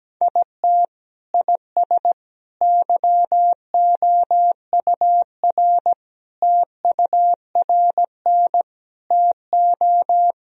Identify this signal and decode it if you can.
Morse code, heard as a keyed tone, about 17 wpm, 700 Hz. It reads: IT IS YOUR TURN TO
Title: Sentences from Top 1000 Words - Review Part 2 - 17wpm